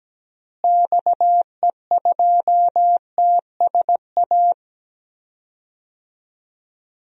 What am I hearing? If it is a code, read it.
Morse code, heard as a keyed tone, about 17 wpm, 700 Hz. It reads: XE2TSA
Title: DXCC-100 Call Signs - 6 Characters - Repeated 2x - 17wpm